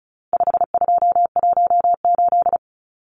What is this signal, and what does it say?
Morse code, heard as a keyed tone, about 34 wpm, 700 Hz. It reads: 5218
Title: Numbers - 4-Digit 35wpm